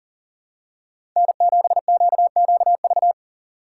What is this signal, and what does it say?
Morse code, heard as a keyed tone, about 40 wpm, 700 Hz. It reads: N7QQV
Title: USA Call Signs - 4x Character Spacing - 40wpm